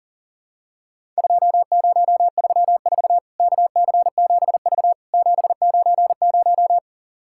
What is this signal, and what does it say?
Morse code, heard as a keyed tone, about 40 wpm, 700 Hz. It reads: 2034 KC7V 790
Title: Worked All Europe DX Contest - QTC Exchange - 40wpm